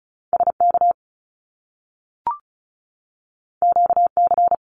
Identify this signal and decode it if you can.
Morse code, heard as a keyed tone, about 36 wpm, 700 Hz. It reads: SK QC
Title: Canadian Provinces and Territories 35wpm